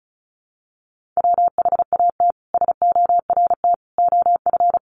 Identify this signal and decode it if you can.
Morse code, heard as a keyed tone, about 34 wpm, 700 Hz. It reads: WHAT SORT OF
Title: Sentences from Top 1300 Words 35wpm